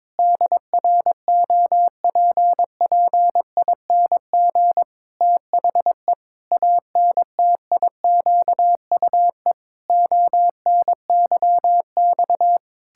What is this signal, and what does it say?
Morse code, heard as a keyed tone, about 22 wpm, 700 Hz. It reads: DROPPING THE ANTIQUE ONYX
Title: Pangram Challenge v2 - 22wpm